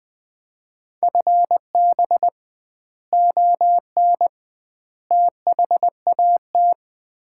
Morse code, heard as a keyed tone, about 20 wpm, 700 Hz. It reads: FB ON THAT
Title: Common QSO Phrases - 2x Word-Spacing - 20wpm